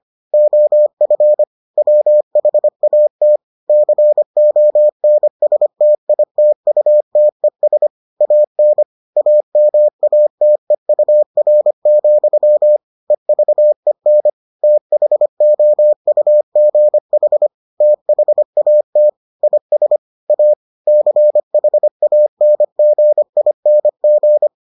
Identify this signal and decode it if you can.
Morse code, heard as a keyed tone, about 25 wpm, 600 Hz. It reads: OF WHAT CONSTITUTES AN AMATEUR, EVEN THOUGH THAT IS A CHANGING